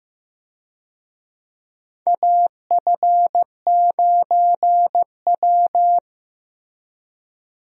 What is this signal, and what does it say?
Morse code, heard as a keyed tone, about 15 wpm, 700 Hz. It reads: AF9W
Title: USA Call Signs - 4 Characters - Repeated 2x - 15wpm